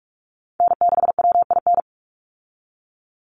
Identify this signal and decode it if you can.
Morse code, heard as a keyed tone, about 44 wpm, 700 Hz. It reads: N6WIN
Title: Call Signs Repeated 2x - 45wpm